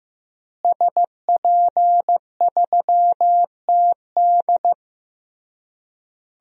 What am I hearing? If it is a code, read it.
Morse code, heard as a keyed tone, about 15 wpm, 700 Hz. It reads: SP3TD
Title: DXCC-100 Call Signs - 5 Characters - Repeated 3x - 15wpm